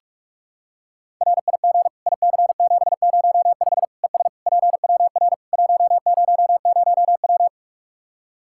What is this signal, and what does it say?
Morse code, heard as a keyed tone, about 46 wpm, 700 Hz. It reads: RIG IC705 ES PWR 100W
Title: Practice QSO - Exchange 2 - Info - 45wpm